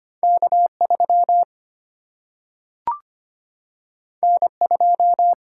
Morse code, heard as a keyed tone, about 25 wpm, 700 Hz. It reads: K 3 N 2
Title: Single-Letter Single-Number 25wpm